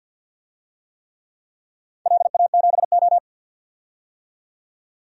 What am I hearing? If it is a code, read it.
Morse code, heard as a keyed tone, about 53 wpm, 700 Hz. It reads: LA7O
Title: DXCC-100 Call Signs - 4 Characters - 50wpm